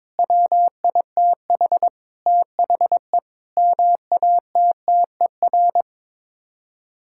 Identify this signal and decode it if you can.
Morse code, heard as a keyed tone, about 22 wpm, 700 Hz. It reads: WITH THE MATTER
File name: practice / 22wpm / Sets of 7 Words - 22wpm